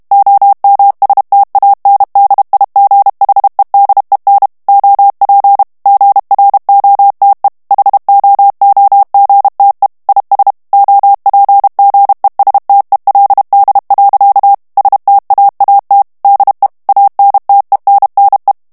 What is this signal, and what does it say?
Morse code, heard as a keyed tone, about 32 wpm, 800 Hz. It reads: OMSTANDIGHEDEN OP GROTE HOOGTE IS OPGESTELD. STAAT DE ANTENNE
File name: morse mp3 / HOLLANDAIS texte clair 1 vitesse 30 mots minute